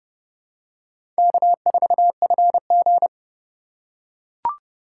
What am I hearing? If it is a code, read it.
Morse code, heard as a keyed tone, about 30 wpm, 700 Hz. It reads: K4FG
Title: USA Call Signs - 4 Characters - 2x Character Spacing - 30wpm